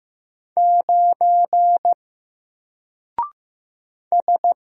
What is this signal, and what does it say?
Morse code, heard as a keyed tone, about 15 wpm, 700 Hz. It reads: 9 S
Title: Single Character (Letter:Number:Punctuation:Prosign) 15wpm